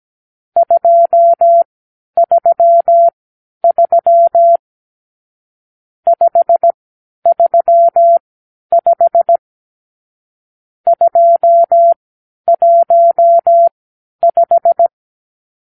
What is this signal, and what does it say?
Morse code, heard as a keyed tone, about 17 wpm, spaced as for 10 wpm, 680 Hz. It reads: 233 535 215
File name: arrl / Review (1-5)